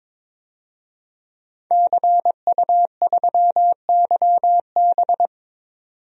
Morse code, heard as a keyed tone, about 22 wpm, 700 Hz. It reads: CU3YB
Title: DXCC-100 Call Signs - 5 Characters - 22wpm